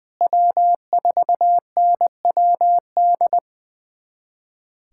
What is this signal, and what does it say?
Morse code, heard as a keyed tone, about 20 wpm, 700 Hz. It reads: W4NWD
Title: USA Callsign DE Callsign - 5 Characters - 20wpm